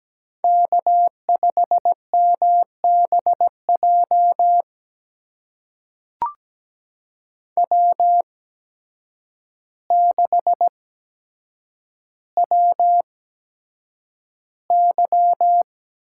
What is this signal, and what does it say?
Morse code, heard as a keyed tone, about 17 wpm, spaced as for 8 wpm, 700 Hz. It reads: K5MBJ W6WY
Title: USA Call Signs - 5 Characters - 8x Character Spacing - 17wpm